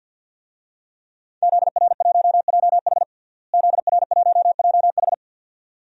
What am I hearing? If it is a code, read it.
Morse code, heard as a keyed tone, about 53 wpm, 700 Hz. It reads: ZR1JH ZR1JH
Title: DXCC-100 Call Signs - 5 Characters - Repeated 2x - 50wpm